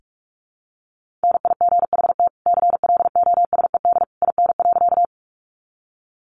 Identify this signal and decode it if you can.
Morse code, heard as a keyed tone, about 44 wpm, 700 Hz. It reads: NIGHT CLOSED IN.
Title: Sentences from Top 1400 Words 45wpm